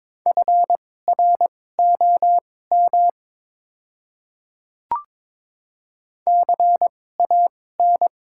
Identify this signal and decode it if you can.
Morse code, heard as a keyed tone, about 22 wpm, spaced as for 16 wpm, 690 Hz. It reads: FROM CAN
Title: 2x Character Spacing - Top 100 Words - 22wpm